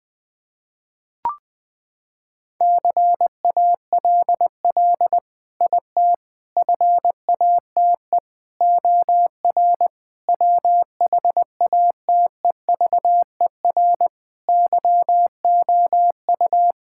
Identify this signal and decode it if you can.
Morse code, heard as a keyed tone, about 20 wpm, 700 Hz. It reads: CALL IT FATE OR WHATEVER YOU
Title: Sentences with 7-Word Phrases - 20wpm